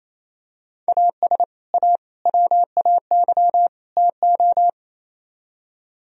Morse code, heard as a keyed tone, about 34 wpm, 700 Hz. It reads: AS A WAY TO
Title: Speed-Racing - Sets of 4 Words 28wpm